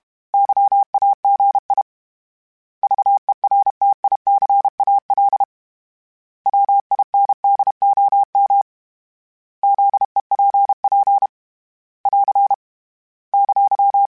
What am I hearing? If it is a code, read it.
Morse code, heard as a keyed tone, about 32 wpm, 800 Hz. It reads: YAGI VERTICAL WINDOM ZEPP <AR> !
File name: fromSMs / FromSMs mp3s / Antennas@30